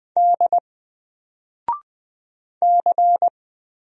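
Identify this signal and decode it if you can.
Morse code, heard as a keyed tone, about 20 wpm, 700 Hz. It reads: D C
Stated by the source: Single letter and number Morse drill at 20 wpm, from N4LCD.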